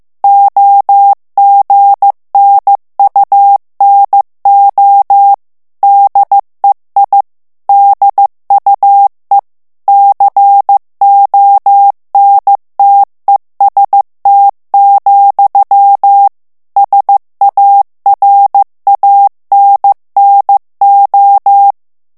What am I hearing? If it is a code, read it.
Morse code, heard as a keyed tone, about 15 wpm, 800 Hz. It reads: OGNUNO DEI DUE CONTEST, SARANNO
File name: morse mp3 / ITALIEN texte clair 2 vitesse 14 mots minute